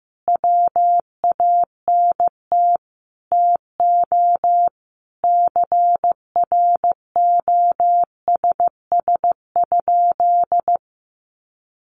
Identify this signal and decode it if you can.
Morse code, heard as a keyed tone, about 15 wpm, 700 Hz. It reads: WANT TO CROSS?
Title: Sentences from Top 300 Words 15wpm